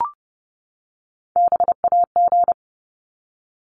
Morse code, heard as a keyed tone, about 30 wpm, 700 Hz. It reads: BAG
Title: Top 1501-1750 Words 30wpm